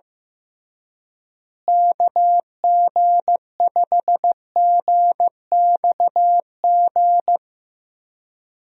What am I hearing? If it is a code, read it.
Morse code, heard as a keyed tone, about 15 wpm, 700 Hz. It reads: KG5GXG
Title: USA Call Signs - 6 Characters - Repeated 2x - 15wpm